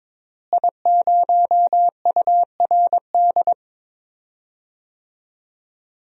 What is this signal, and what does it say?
Morse code, heard as a keyed tone, about 21 wpm, 700 Hz. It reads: I0URD
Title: DXCC-100 Call Signs - 5 Characters - Repeated 3x - 22wpm